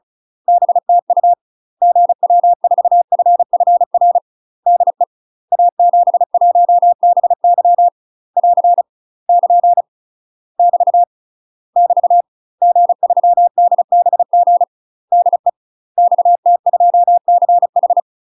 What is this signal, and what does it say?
Morse code, heard as a keyed tone, about 35 wpm, 700 Hz. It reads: BTU GW4FFR DE A71BY <AR> <KN> <BT> <BT> G3DBG DE XT2CH
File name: UK MASTER QSOs TRAINING / 35WPM / UK MASTER QSOs 0020